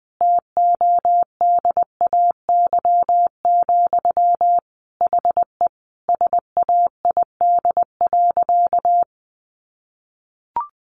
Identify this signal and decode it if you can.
Morse code, heard as a keyed tone, about 20 wpm, 700 Hz. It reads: TODAY, HE SAID.
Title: Sentences from Top 600 Words 20wpm